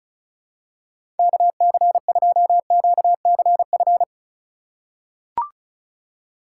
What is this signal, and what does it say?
Morse code, heard as a keyed tone, about 36 wpm, 700 Hz. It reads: KC2QCF
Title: USA Call Signs - 4x Character Spacing - 35wpm